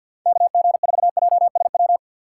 Morse code, heard as a keyed tone, about 50 wpm, 700 Hz. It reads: KG4JSW
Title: USA Call Signs - 6 Characters - 3x Character Spacing - 50wpm